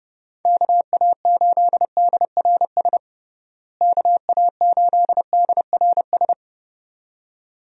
Morse code, heard as a keyed tone, about 29 wpm, 700 Hz. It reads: KA8DRS KA8DRS
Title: USA Call Signs - 6 Characters - Repeated 2x - 30wpm